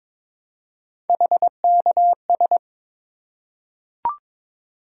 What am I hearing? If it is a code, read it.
Morse code, heard as a keyed tone, about 22 wpm, 700 Hz. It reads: HKS
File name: practice / 22wpm / Three Letter-Number Combinations - 22wpm